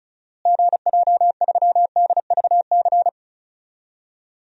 Morse code, heard as a keyed tone, about 34 wpm, 700 Hz. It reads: GJ3DVC
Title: DXCC-100 Call Signs - 6 Characters - Repeated 2x - 35wpm